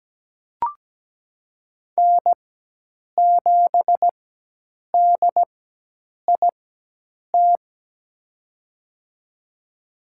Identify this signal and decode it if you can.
Morse code, heard as a keyed tone, about 17 wpm, spaced as for 8 wpm, 700 Hz. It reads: N7DIT
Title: USA Call Signs - 5 Characters - 4x Character Spacing - 17wpm